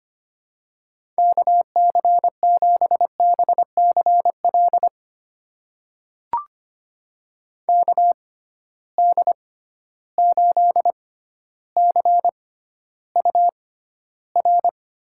Morse code, heard as a keyed tone, about 25 wpm, spaced as for 13 wpm, 700 Hz. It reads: KC7BCL KD8CUR
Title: USA Call Signs - 6 Characters - 6x Character Spacing - 25wpm